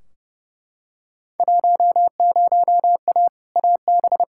Morse code, heard as a keyed tone, about 32 wpm, 700 Hz. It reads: 10A AB
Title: Field Day Exchanges Encore - 30wpm